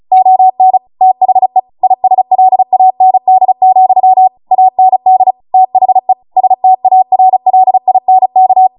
Morse code, heard as a keyed tone, about 36 wpm, 750 Hz. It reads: ON THE ISLAND, AND THE STARLINK